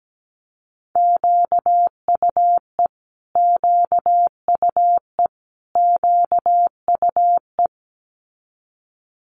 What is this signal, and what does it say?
Morse code, heard as a keyed tone, about 17 wpm, 700 Hz. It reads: QUE QUE QUE
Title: Multi-letter Phonemes Repeated 3x 17wpm